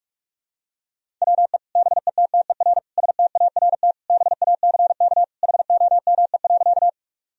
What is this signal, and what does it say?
Morse code, heard as a keyed tone, about 46 wpm, 700 Hz. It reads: WE BETTER START BACK HOME.
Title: Sentences from Top 1000 Words - Review Part 1 - 45wpm